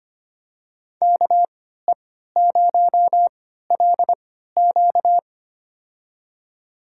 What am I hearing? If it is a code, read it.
Morse code, heard as a keyed tone, about 25 wpm, spaced as for 14 wpm, 700 Hz. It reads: KE0LQ
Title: USA Call Signs - 5 Characters - 3x Character Spacing - 25wpm